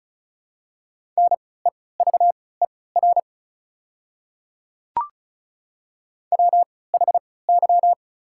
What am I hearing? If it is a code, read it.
Morse code, heard as a keyed tone, about 34 wpm, spaced as for 20 wpm, 700 Hz. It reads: NEVER WHY
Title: 3x Character Spacing - Top 300 Words - 35wpm